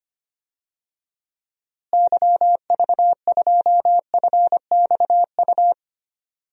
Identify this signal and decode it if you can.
Morse code, heard as a keyed tone, about 25 wpm, 700 Hz. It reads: YV2FXU
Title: DXCC-100 Call Signs - 6 Characters - Repeated 2x - 25wpm